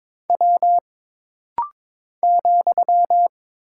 Morse code, heard as a keyed tone, about 22 wpm, 700 Hz. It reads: W ,
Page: This rapid-fire timing is geared to push your ICR (Instant Character Recognition) skills.